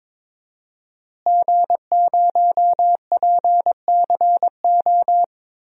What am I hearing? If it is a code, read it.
Morse code, heard as a keyed tone, about 22 wpm, 700 Hz. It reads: G0PCO
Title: DXCC-100 Call Signs - 5 Characters - Repeated 3x - 22wpm